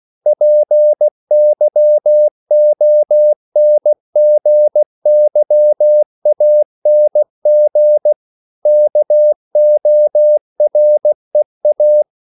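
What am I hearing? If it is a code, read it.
Morse code, heard as a keyed tone, about 16 wpm, 600 Hz. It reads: PYONGYANG KOREA
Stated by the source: CW practice of national capitals at 16 words per minute, from ON6ZQ.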